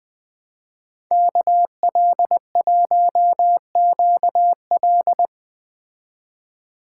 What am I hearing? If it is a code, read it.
Morse code, heard as a keyed tone, about 20 wpm, 700 Hz. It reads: KL1QL